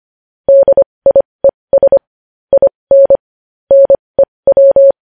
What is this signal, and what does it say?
Morse code, heard as a keyed tone, about 25 wpm, spaced as for 20 wpm, 550 Hz. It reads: DIES IN NEW
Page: News Headlines in Morse Code at 20 WPM